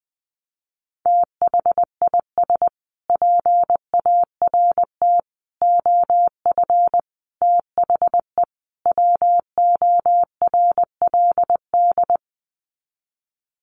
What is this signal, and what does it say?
Morse code, heard as a keyed tone, about 20 wpm, 700 Hz. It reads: THIS PART OF THE WORLD
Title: Sets of 5 Words Encore2 20wpm